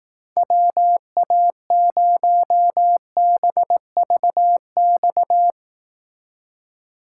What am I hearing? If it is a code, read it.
Morse code, heard as a keyed tone, about 18 wpm, 700 Hz. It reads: WA0BVX
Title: Speed Racing - USA Call Signs 15wpm